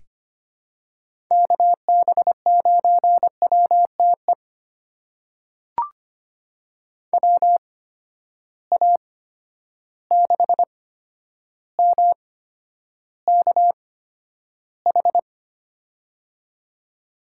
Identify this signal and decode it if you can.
Morse code, heard as a keyed tone, about 25 wpm, spaced as for 11 wpm, 700 Hz. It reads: KB9WTE WA6MKH
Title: USA Call Signs - 6 Characters - 8x Character Spacing - 25wpm